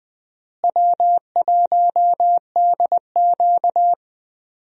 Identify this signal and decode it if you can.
Morse code, heard as a keyed tone, about 20 wpm, 700 Hz. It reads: W1DQ
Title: USA Call Signs - 4 Characters - Repeated 2x - 20wpm